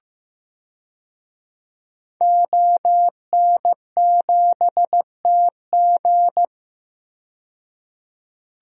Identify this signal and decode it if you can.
Morse code, heard as a keyed tone, about 15 wpm, 700 Hz. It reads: ON7TG
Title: DXCC-100 Call Signs - 5 Characters - 15wpm